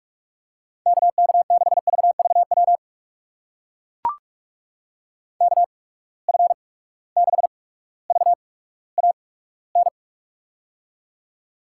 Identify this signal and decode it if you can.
Morse code, heard as a keyed tone, about 45 wpm, spaced as for 19 wpm, 700 Hz. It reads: KK6VVW KF6VAN
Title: USA Call Signs - 6 Characters - 8x Character Spacing - 45wpm